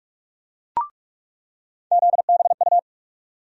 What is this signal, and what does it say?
Morse code, heard as a keyed tone, about 44 wpm, 700 Hz. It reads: ZBU
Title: USA Call Sign Postfixes - 3 Letters 45wpm